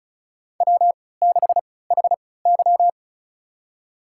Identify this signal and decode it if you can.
Morse code, heard as a keyed tone, about 34 wpm, spaced as for 20 wpm, 700 Hz. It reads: W6HY
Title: USA Call Signs - 4 Characters - 3x Character Spacing - 35wpm